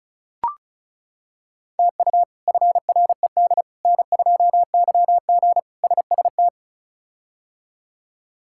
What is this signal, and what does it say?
Morse code, heard as a keyed tone, about 36 wpm, 700 Hz. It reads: TU FRED N2YG SST
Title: K1USN SST QSO Practice - 35wpm